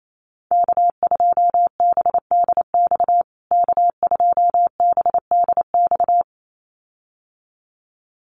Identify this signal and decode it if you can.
Morse code, heard as a keyed tone, about 28 wpm, 700 Hz. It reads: K2BDX K2BDX
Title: USA Call Signs Repeated 2x 28wpm